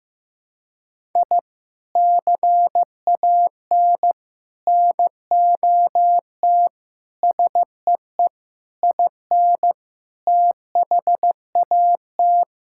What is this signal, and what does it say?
Morse code, heard as a keyed tone, about 15 wpm, 700 Hz. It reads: I CAN NOT SEE IN THAT
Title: Sentences from Top 1000 Words - Review Part 5 - 15wpm